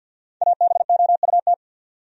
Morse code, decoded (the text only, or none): ABOUT